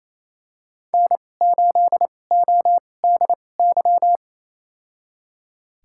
Morse code, heard as a keyed tone, about 28 wpm, spaced as for 20 wpm, 700 Hz. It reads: N8ODY
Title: USA Call Signs - 5 Characters - 2x Character Spacing - 28wpm